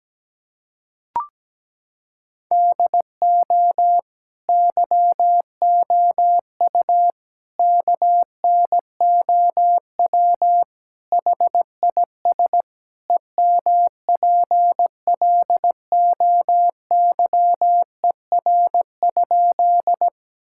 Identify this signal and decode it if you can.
Morse code, heard as a keyed tone, about 17 wpm, 700 Hz. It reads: DO YOU KNOW HIS EMPLOYER?
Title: Sentences from Top 1600 Words 17wpm